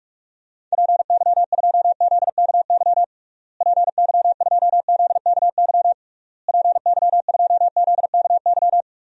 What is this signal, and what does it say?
Morse code, heard as a keyed tone, about 46 wpm, 700 Hz. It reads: PY2ZKY PY2ZKY PY2ZKY
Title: DXCC-100 Call Signs - 6 Characters - Repeated 3x - 45wpm